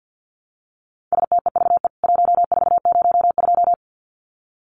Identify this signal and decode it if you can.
Morse code, heard as a keyed tone, about 50 wpm, 700 Hz. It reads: STEVE 1402
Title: CWT Exchange 50wpm